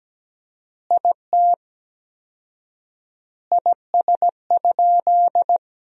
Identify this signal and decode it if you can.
Morse code, heard as a keyed tone, about 17 wpm, 700 Hz. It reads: IT IS?
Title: 4x Word-Spacing - Sentences from Top 100 Words - 17wpm